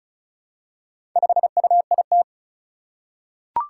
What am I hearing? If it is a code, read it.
Morse code, heard as a keyed tone, about 36 wpm, 690 Hz. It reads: 5UIT
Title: Four Challenging Letter-Number Combinations 35wpm